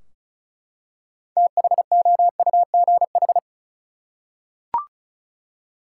Morse code, heard as a keyed tone, about 36 wpm, 700 Hz. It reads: THOUGH
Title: Six Letter Words - 35wpm